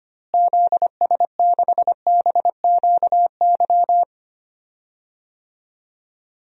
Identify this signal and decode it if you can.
Morse code, heard as a keyed tone, about 25 wpm, 700 Hz. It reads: ZS6BQY
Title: DXCC-100 Call Signs - 6 Characters - Repeated 3x - 25wpm